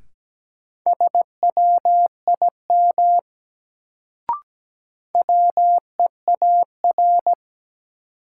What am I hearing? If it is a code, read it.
Morse code, heard as a keyed tone, about 17 wpm, 700 Hz. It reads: SWIM WEAR
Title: Sentences from Top 1000 Words - Review Part 6 - New Word Drill - 17wpm